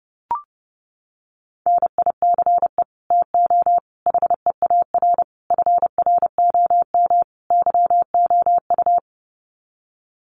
Morse code, heard as a keyed tone, about 30 wpm, 700 Hz. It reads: NICE TO HEAR FROM YOU
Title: Common QSO Phrases 30wpm